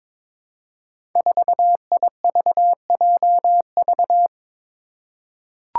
Morse code, heard as a keyed tone, about 22 wpm, 700 Hz. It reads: 4IVJV
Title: Five Challenging Letter-Number Combinations 22wpm